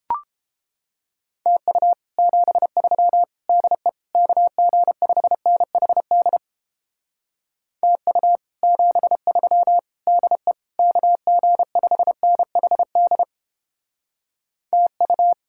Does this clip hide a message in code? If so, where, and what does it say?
Yes: Morse code from 1.5 to 15.3 s, heard as a keyed tone, about 29 wpm, 700 Hz. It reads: TU 73 DE KG5NHD TU 73 DE KG5NHD TU